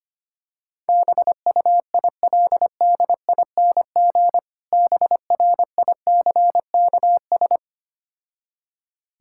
Morse code, heard as a keyed tone, about 25 wpm, 690 Hz. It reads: BUILDING BRICKS
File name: practice / 25wpm / Difficult Words with Hints - 25wpm